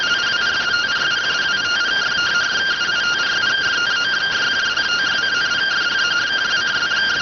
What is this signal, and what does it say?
Two-tone teleprinter signal, about 170 Hz shift.